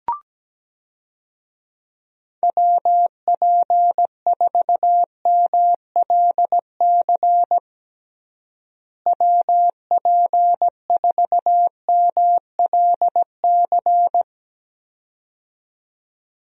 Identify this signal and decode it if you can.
Morse code, heard as a keyed tone, about 17 wpm, 700 Hz. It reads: WP4MLC WP4MLC
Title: USA Call Signs - 6 Characters - Repeated 2x - 17wpm